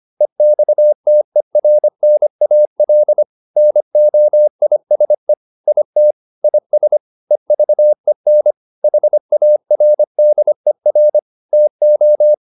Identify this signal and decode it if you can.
Morse code, heard as a keyed tone, about 25 wpm, 600 Hz. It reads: EXTERNAL NOISE IT IS EVEN HARDER TO